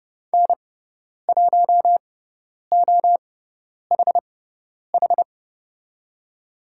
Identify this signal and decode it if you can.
Morse code, heard as a keyed tone, about 30 wpm, spaced as for 10 wpm, 690 Hz. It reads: N1OHH
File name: Lesson 17g - Letter u - Call Signs 30wpm